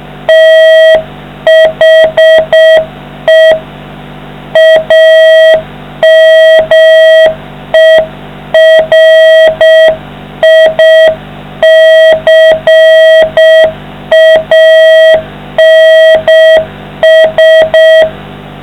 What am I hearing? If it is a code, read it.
Morse code, heard as a keyed tone, about 6 wpm, 660 Hz, with noise. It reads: THE AMERICANS